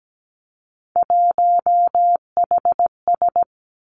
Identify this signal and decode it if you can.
Morse code, heard as a keyed tone, about 17 wpm, 700 Hz. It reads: 1HS
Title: Three Challenging Letter-Number Combinations 17wpm